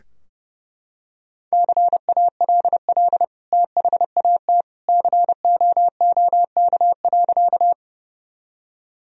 Morse code, heard as a keyed tone, about 28 wpm, 700 Hz. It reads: CALL THAT COOK.
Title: Sentences from Top 1000 Words - Review Part 4 - 30wpm